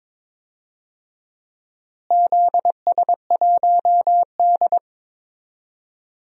Morse code, heard as a keyed tone, about 22 wpm, 700 Hz. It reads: ZS1D
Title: DXCC-100 Call Signs - 4 Characters - Repeated 2x - 22wpm